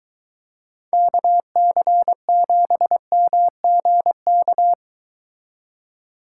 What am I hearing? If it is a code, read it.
Morse code, heard as a keyed tone, about 23 wpm, 700 Hz. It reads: KC7MGK